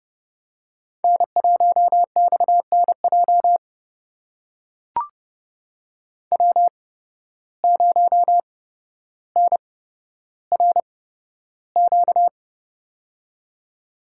Morse code, heard as a keyed tone, about 29 wpm, spaced as for 13 wpm, 700 Hz. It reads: N1XNJ W0NRQ
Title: USA Call Signs - 5 Characters - 8x Character Spacing - 30wpm